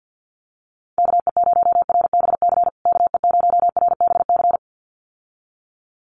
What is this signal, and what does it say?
Morse code, heard as a keyed tone, about 50 wpm, 700 Hz. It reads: KE0RBC KE0RBC
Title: USA Call Signs Repeated 2x 50wpm